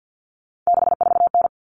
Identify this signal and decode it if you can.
Morse code, heard as a keyed tone, about 50 wpm, 700 Hz. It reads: 64N
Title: Common Cut Numbers - 3-Digit 50wpm